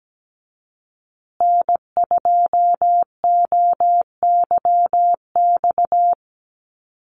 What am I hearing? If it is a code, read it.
Morse code, heard as a keyed tone, about 17 wpm, 700 Hz. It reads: N2OYX
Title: USA Call Signs Repeated 3x 17wpm